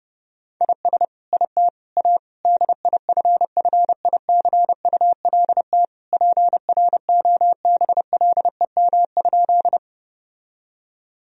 Morse code, heard as a keyed tone, about 32 wpm, 700 Hz. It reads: IS IT A DIFFICULT PROBLEM?
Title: Sentences from Top 1000 Words - Review Part 2 - 30wpm